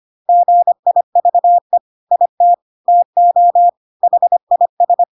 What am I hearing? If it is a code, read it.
Morse code, heard as a keyed tone, about 25 wpm, 700 Hz. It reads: GIVE IT TO HIS
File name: Pinocchio / Pinocchio ch2 25 25 ik1hge